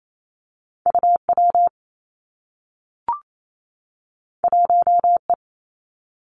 Morse code, heard as a keyed tone, about 28 wpm, 700 Hz. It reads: U W 1 E
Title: Two Challenging Letter-Number Combinations 28wpm